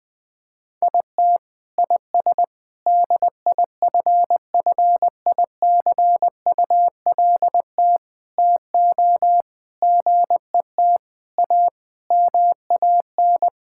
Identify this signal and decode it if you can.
Morse code, heard as a keyed tone, about 21 wpm, 700 Hz. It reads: IT IS DIFFICULT TO GET A MAN
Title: Sets of 9 Words - 20wpm